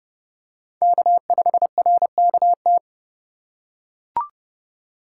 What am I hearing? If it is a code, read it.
Morse code, heard as a keyed tone, about 28 wpm, 700 Hz. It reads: K5RKT